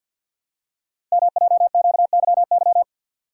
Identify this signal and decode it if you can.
Morse code, heard as a keyed tone, about 52 wpm, 700 Hz. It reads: MJQYY
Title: Code Groups - 5 Letters - 50wpm